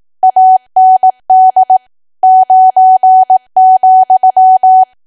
Morse code, heard as a keyed tone, about 18 wpm, 750 Hz. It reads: AND 9,
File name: cwa / W1AW-Level2 / 18WPM